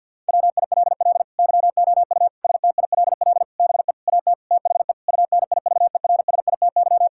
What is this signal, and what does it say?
Morse code, heard as a keyed tone, about 52 wpm, 700 Hz. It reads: WILL YOU STILL BE AT THE UNIVERSITY
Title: Sentences from Top 1000 Words - Review Part 5 - 50wpm